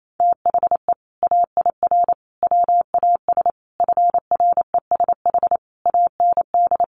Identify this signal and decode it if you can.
Morse code, heard as a keyed tone, about 28 wpm, 700 Hz. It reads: THE AIR WAS FRESH AND
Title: Sentences from Top 1200 Words 28wpm